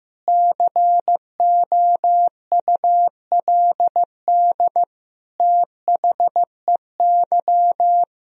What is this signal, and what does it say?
Morse code, heard as a keyed tone, about 15 wpm, 700 Hz. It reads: COULD THEY
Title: Sentences from Top 1000 Words - Review Part 3 - 15wpm